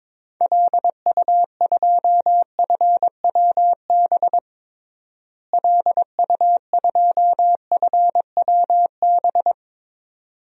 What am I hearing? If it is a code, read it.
Morse code, heard as a keyed tone, about 22 wpm, 700 Hz. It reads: LU2FWB LU2FWB
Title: DXCC-100 Call Signs - 6 Characters - Repeated 2x - 22wpm